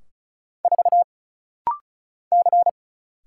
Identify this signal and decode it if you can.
Morse code, heard as a keyed tone, about 35 wpm, 700 Hz. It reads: 4 C
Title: Single Letter-Number Alt - Rapid-Fire 35wpm